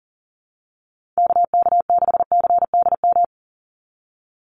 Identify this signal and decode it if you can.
Morse code, heard as a keyed tone, about 40 wpm, 700 Hz. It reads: KK6CDM
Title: USA Call Signs 40wpm